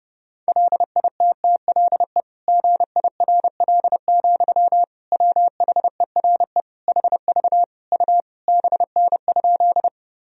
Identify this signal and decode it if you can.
Morse code, heard as a keyed tone, about 30 wpm, 700 Hz. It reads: LITTLE GIRL, WHERE HV U BN?
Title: Sentences from Top 400 Words and Abbreviations 30wpm